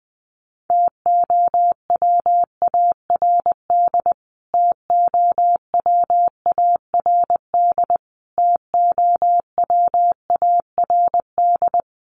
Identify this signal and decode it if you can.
Morse code, heard as a keyed tone, about 20 wpm, 700 Hz. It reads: TOWARD TOWARD TOWARD
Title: Top 500 Words Repeated 20wpm